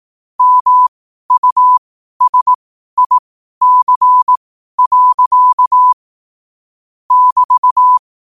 Morse code, heard as a keyed tone, about 18 wpm, spaced as for 12 wpm, 1000 Hz. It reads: MUSIC. <BT>
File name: qotdfiles / 12wpm